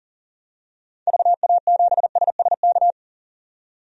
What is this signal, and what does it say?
Morse code, heard as a keyed tone, about 40 wpm, 700 Hz. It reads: VA7SSK
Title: DXCC-100 Call Signs - 6 Characters - 40wpm